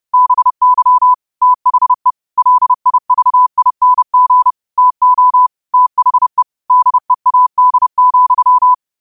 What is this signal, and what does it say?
Morse code, heard as a keyed tone, about 30 wpm, 1000 Hz. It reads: BY THE LIVING TO THE DEAD,
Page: Quotes for Sun, 17 Aug 2025 in Morse Code at 30 words per minute.